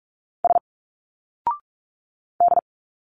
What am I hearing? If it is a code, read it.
Morse code, heard as a keyed tone, about 46 wpm, 700 Hz. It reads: S D
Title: Single Letters 45wpm